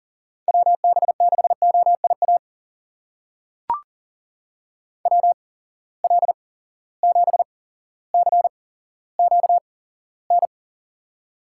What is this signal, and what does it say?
Morse code, heard as a keyed tone, about 40 wpm, spaced as for 17 wpm, 700 Hz. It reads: WB6OIA WL7CQN
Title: USA Call Signs - 6 Characters - 8x Character Spacing - 40wpm